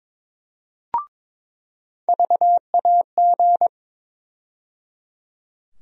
Morse code, heard as a keyed tone, about 22 wpm, 700 Hz. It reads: VAG
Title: USA Call Sign Postfixes - 3 Letters 22wpm